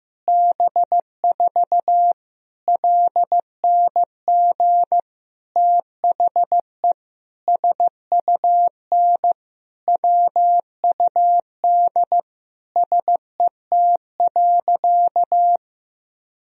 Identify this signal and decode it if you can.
Morse code, heard as a keyed tone, about 15 wpm, 700 Hz. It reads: B4 LNG THE SUN WUD SET.
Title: Sentences from Top 400 Words and Abbreviations 15wpm